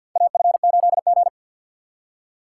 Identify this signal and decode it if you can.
Morse code, heard as a keyed tone, about 52 wpm, 700 Hz. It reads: AF8G
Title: USA Call Signs - 4 Characters - 2x Character Spacing - 50wpm